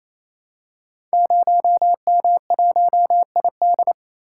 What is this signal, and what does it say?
Morse code, heard as a keyed tone, about 27 wpm, 700 Hz. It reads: 0M1ID
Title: Code Groups - 5 Letter:Numbers - 28wpm